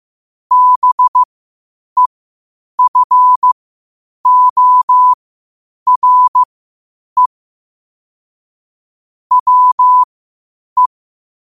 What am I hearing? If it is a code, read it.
Morse code, heard as a keyed tone, about 15 wpm, spaced as for 8 wpm, 1000 Hz. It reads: BEFORE WE
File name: qotdfiles / 8wpm